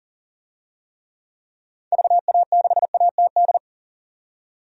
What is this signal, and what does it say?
Morse code, heard as a keyed tone, about 40 wpm, 700 Hz. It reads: VA6ATD
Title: DXCC-100 Call Signs - 6 Characters - 40wpm